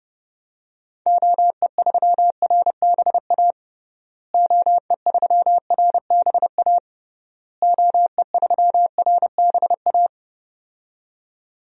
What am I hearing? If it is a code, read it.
Morse code, heard as a keyed tone, about 32 wpm, 700 Hz. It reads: OE3RBA OE3RBA OE3RBA
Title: DXCC-100 Call Signs - 6 Characters - Repeated 3x - 30wpm